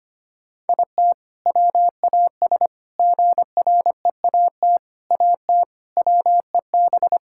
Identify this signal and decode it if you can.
Morse code, heard as a keyed tone, about 25 wpm, 700 Hz. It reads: IT WAS GREAT AT WEB
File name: practice / 25wpm / All-Age Appropriate Jokes - 25wpm